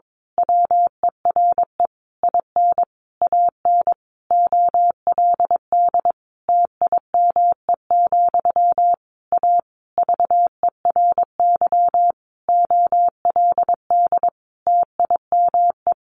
Morse code, heard as a keyed tone, about 22 wpm, 700 Hz. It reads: WERE IN AN OLD TIME, A VERY OLD TIME
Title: A Sound Of Thunder 22wpm